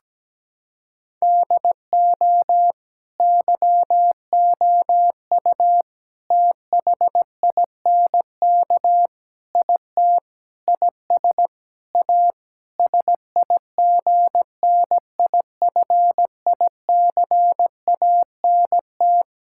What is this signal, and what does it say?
Morse code, heard as a keyed tone, about 17 wpm, 700 Hz. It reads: DO YOU THINK IT IS A SIGNIFICANT
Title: Sentences from Top 1000 Words - Review Part 6 - 17wpm